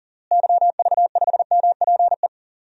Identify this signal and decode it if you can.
Morse code, heard as a keyed tone, about 40 wpm, 700 Hz. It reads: YV5MPE
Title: DXCC-100 Call Signs - 6 Characters - Repeated 3x - 40wpm